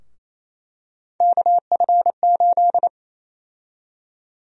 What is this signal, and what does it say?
Morse code, heard as a keyed tone, about 28 wpm, 700 Hz. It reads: KF8
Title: USA Call Sign Prefixes - 28wpm